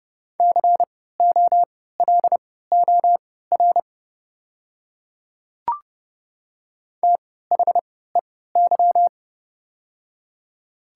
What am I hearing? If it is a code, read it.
Morse code, heard as a keyed tone, about 30 wpm, spaced as for 17 wpm, 700 Hz. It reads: COLOR THEY
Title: 3x Character Spacing - Top 300 Words - 30wpm